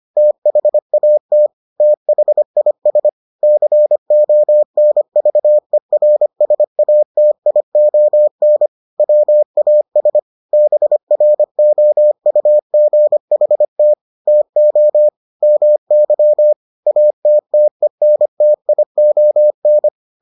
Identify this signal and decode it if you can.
Morse code, heard as a keyed tone, about 25 wpm, 600 Hz. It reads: THAT THIS CONVERSATION WAS BROUGHT TO MY ATTENTION